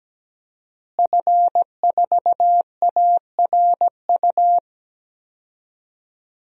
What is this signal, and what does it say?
Morse code, heard as a keyed tone, about 17 wpm, 700 Hz. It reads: F4ARU
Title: DXCC-100 Call Signs - 5 Characters - Repeated 3x - 17wpm